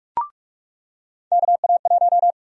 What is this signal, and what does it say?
Morse code, heard as a keyed tone, about 46 wpm, 700 Hz. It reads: KA1
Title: USA Call Sign Prefixes - 45wpm